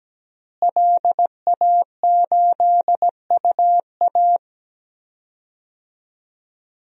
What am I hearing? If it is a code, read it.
Morse code, heard as a keyed tone, about 17 wpm, 700 Hz. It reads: LA8UA
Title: DXCC-100 Call Signs - 5 Characters - Repeated 3x - 17wpm